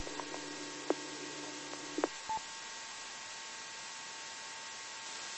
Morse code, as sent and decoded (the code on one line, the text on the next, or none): none